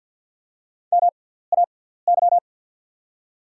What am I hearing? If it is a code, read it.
Morse code, heard as a keyed tone, about 50 wpm, spaced as for 17 wpm, 700 Hz. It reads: MAY
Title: 6x Character Spacing - Top 100 Words - 50wpm